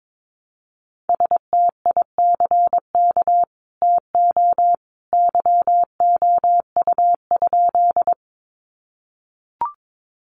Morse code, heard as a keyed tone, about 22 wpm, 700 Hz. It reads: STICK TO YOU?
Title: Sentences from Top 1200 Words 22wpm